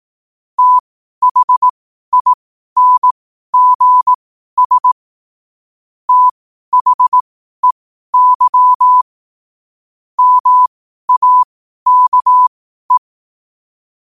Morse code, heard as a keyed tone, about 18 wpm, spaced as for 12 wpm, 1000 Hz. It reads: THINGS THEY MAKE
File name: qotdfiles / 12wpm